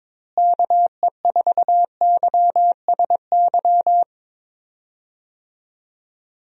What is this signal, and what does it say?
Morse code, heard as a keyed tone, about 22 wpm, 700 Hz. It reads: KE4YSY
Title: USA Call Signs - 6 Characters - Repeated 2x - 22wpm